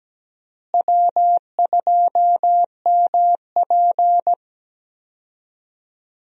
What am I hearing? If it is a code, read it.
Morse code, heard as a keyed tone, about 17 wpm, 700 Hz. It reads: W2MP
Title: USA Call Signs - 4 Characters - Repeated 3x - 17wpm